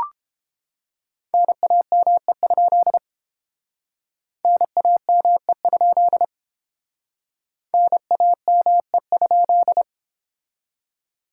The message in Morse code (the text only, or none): NAME? NAME? NAME?